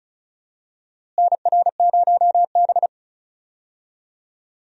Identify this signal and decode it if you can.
Morse code, heard as a keyed tone, about 36 wpm, 700 Hz. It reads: NR0B
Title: USA Call Signs - 4 Characters - Repeated 3x - 35wpm